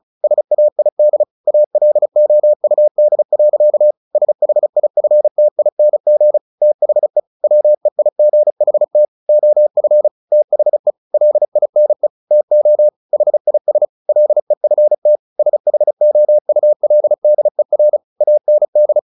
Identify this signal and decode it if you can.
Morse code, heard as a keyed tone, about 34 wpm, 600 Hz. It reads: SAID ALOUD. SHIFTING THE WEIGHT OF THE LINE TO HIS LEFT SHOULDER AND
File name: the old man and the sea / the old man and the sea 1 35wpm